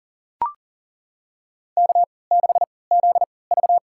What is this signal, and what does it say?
Morse code, heard as a keyed tone, about 40 wpm, spaced as for 23 wpm, 700 Hz. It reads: K6ZV
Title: USA Call Signs - 4 Characters - 3x Character Spacing - 40wpm